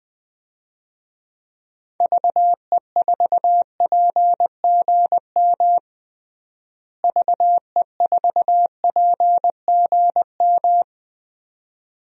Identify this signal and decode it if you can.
Morse code, heard as a keyed tone, about 20 wpm, 700 Hz. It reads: VE4PGM VE4PGM
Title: DXCC-100 Call Signs - 6 Characters - Repeated 2x - 20wpm